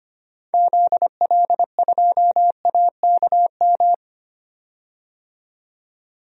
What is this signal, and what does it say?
Morse code, heard as a keyed tone, about 25 wpm, 700 Hz. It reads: ZL2AKM
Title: DXCC-100 Call Signs - 6 Characters - 25wpm